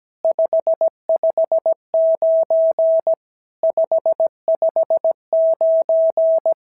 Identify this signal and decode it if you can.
Morse code, heard as a keyed tone, about 17 wpm, 650 Hz. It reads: 559 559
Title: SOTA QSO Practice 17wpm